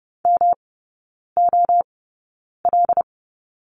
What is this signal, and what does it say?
Morse code, heard as a keyed tone, about 30 wpm, spaced as for 9 wpm, 700 Hz. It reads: MOL
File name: Lesson 18e - Letter m - Three Characters 30wpm